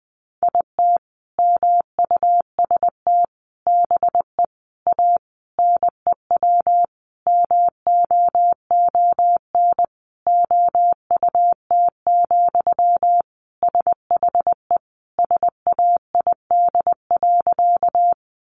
Morse code, heard as a keyed tone, about 20 wpm, 700 Hz. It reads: IT MUST BE A NEW MOON OUT, SHE SAID.
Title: Sentences from Top 500 Words 20wpm